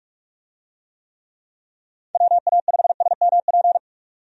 Morse code, heard as a keyed tone, about 46 wpm, 700 Hz. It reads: WA5SMP
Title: USA Call Signs - 6 Characters - 45wpm